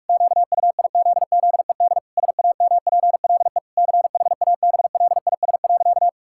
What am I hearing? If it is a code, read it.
Morse code, heard as a keyed tone, about 46 wpm, 700 Hz. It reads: QUIZZED SAMPLE CHABLIS.
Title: Pangram Challenge v2 - 45wpm